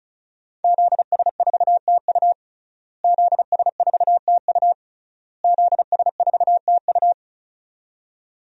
Morse code, heard as a keyed tone, about 36 wpm, 700 Hz. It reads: ZS4TU ZS4TU ZS4TU
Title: DXCC-100 Call Signs - 5 Characters - Repeated 3x - 35wpm